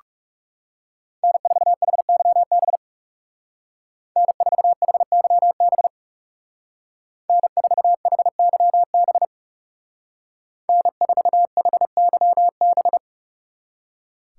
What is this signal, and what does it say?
Morse code, heard as a keyed tone, about 39 wpm, 700 Hz. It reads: N4HYB N4HYB N4HYB N4HYB